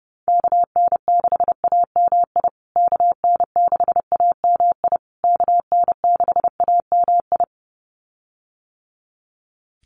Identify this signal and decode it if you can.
Morse code, heard as a keyed tone, about 30 wpm, 700 Hz. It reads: KN6AMI KN6AMI KN6AMI
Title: USA Call Signs Repeated 3x 30wpm